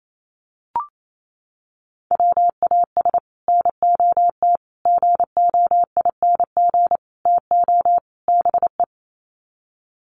Morse code, heard as a keyed tone, about 28 wpm, 700 Hz. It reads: WAS NOT GOING TO BE
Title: Sets of 5 Words Encore 28wpm